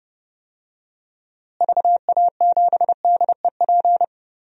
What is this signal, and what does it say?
Morse code, heard as a keyed tone, about 30 wpm, 700 Hz. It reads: VA7DEP
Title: DXCC-100 Call Signs - 6 Characters - Repeated 3x - 30wpm